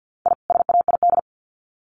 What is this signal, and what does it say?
Morse code, heard as a keyed tone, about 50 wpm, 700 Hz. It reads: I SAID
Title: Sets of 2 Words 50wpm